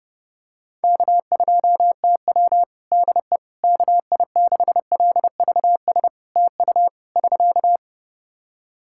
Morse code, heard as a keyed tone, about 30 wpm, 700 Hz. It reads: K2TW DE KI6LVS TU <SK>
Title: Practice QSO - Exchange 4 - Ending - 30wpm